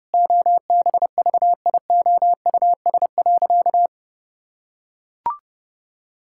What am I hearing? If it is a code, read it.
Morse code, heard as a keyed tone, about 32 wpm, 700 Hz. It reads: OBVIOUS.
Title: Sentences from Top 1600 Words 30wpm